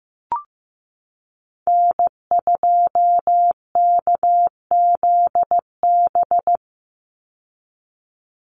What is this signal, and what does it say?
Morse code, heard as a keyed tone, about 15 wpm, 700 Hz. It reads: N2KZB
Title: USA Call Signs 15wpm